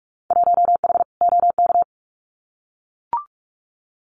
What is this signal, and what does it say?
Morse code, heard as a keyed tone, about 44 wpm, 700 Hz. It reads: 1H OK
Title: Winter Field Day Exchanges 45wpm